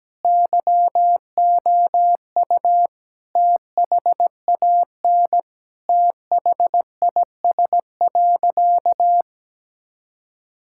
Morse code, heard as a keyed tone, about 17 wpm, 700 Hz. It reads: YOU THAN THIS.
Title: Sentences from Top 1000 Words - Review Part 6 - 17wpm